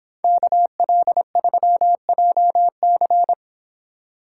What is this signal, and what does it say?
Morse code, heard as a keyed tone, about 26 wpm, 700 Hz. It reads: KL3JC